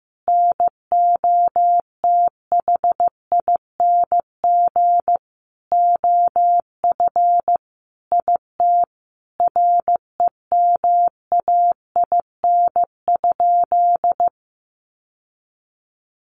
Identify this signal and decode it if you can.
Morse code, heard as a keyed tone, about 15 wpm, 700 Hz. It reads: NOTHING OF IT REMAIN?
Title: Sentences from Top 1200 Words 15wpm